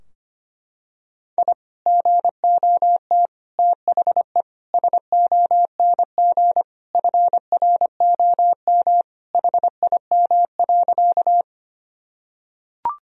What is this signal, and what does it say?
Morse code, heard as a keyed tone, about 25 wpm, 700 Hz. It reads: I GOT THE SONG FROM HIM.
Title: Sentences from Top 1000 Words - Review Part 2 - 25wpm